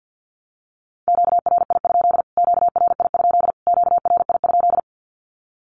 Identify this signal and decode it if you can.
Morse code, heard as a keyed tone, about 48 wpm, 700 Hz. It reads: QRI? QRI? QRI?
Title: Q-Codes Repeated 3x 50wpm